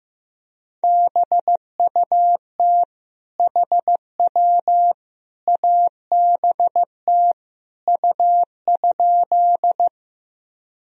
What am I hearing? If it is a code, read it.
Morse code, heard as a keyed tone, about 15 wpm, 700 Hz. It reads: BUT HW ABT U?
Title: Sentences from Top 100 Words and Abbreviations 15wpm